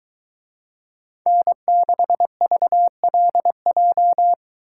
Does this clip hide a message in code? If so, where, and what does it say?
Yes: Morse code from 1.3 to 4.3 s, heard as a keyed tone, about 22 wpm, 700 Hz. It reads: N6VLJ